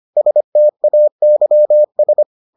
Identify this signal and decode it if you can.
Morse code, heard as a keyed tone, about 25 wpm, 600 Hz. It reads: STAYS